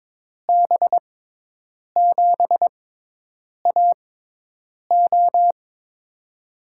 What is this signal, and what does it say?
Morse code, heard as a keyed tone, about 22 wpm, spaced as for 8 wpm, 700 Hz. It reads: B7AO
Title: USA Call Signs - 5 Characters - 6x Character Spacing - 22wpm